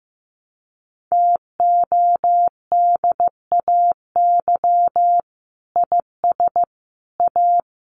Morse code, heard as a keyed tone, about 15 wpm, 700 Hz. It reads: TODAY IS A
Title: Sentences from Top 600 Words 15wpm